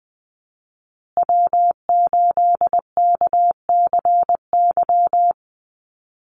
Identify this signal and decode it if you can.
Morse code, heard as a keyed tone, about 20 wpm, 700 Hz. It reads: W8KCY
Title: USA Call Signs Repeated 2x 20wpm